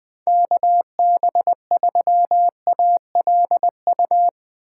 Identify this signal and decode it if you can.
Morse code, heard as a keyed tone, about 20 wpm, 700 Hz. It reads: KB3ALU
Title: USA Call Signs - 6 Characters - 6x Character Spacing - 20wpm